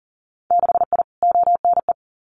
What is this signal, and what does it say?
Morse code, heard as a keyed tone, about 42 wpm, 700 Hz. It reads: 6I ONE
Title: Winter Field Day Exchanges Encore 40wpm